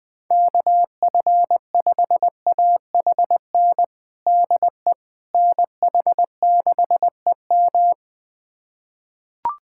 Morse code, heard as a keyed tone, about 20 wpm, 700 Hz. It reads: KF5AHN DE NH6EM
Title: USA Callsign DE Callsign 20wpm